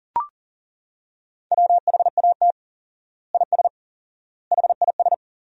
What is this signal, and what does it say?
Morse code, heard as a keyed tone, about 40 wpm, 700 Hz. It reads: WHAT IS HIS
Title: 4x Word-Spacing - Sentences from Top 200 Words - 40wpm